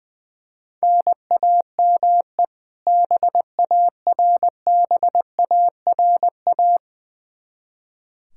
Morse code, heard as a keyed tone, about 20 wpm, 700 Hz. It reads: NAME BARBARA
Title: Practice QSO - Exchange 1 - Beginning - 20wpm